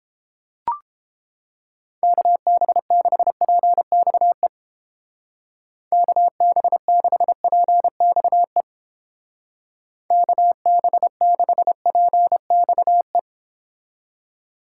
Morse code, heard as a keyed tone, about 30 wpm, 700 Hz. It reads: KB6PXE KB6PXE KB6PXE